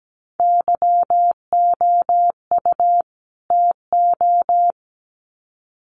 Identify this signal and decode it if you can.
Morse code, heard as a keyed tone, about 17 wpm, 700 Hz. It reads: YOU TO
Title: Sets of 2 Words 17wpm